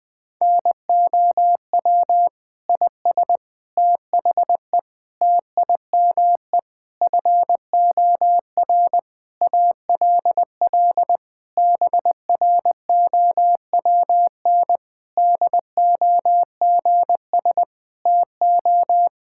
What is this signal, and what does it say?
Morse code, heard as a keyed tone, about 20 wpm, 700 Hz. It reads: NOW IS THE TIME FOR ALL BROWN DOGS TO
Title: Pangram Challenge v2 - 20wpm